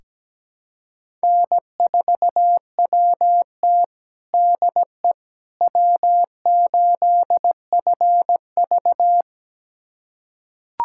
Morse code, heard as a keyed tone, about 17 wpm, 700 Hz. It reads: N4WT DE W8FV
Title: USA Callsign DE Callsign - 4 Characters - 17wpm